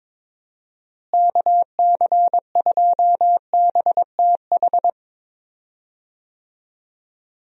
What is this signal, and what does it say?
Morse code, heard as a keyed tone, about 22 wpm, 700 Hz. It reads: KC2BTH
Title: USA Call Signs - 6 Characters - Repeated 2x - 22wpm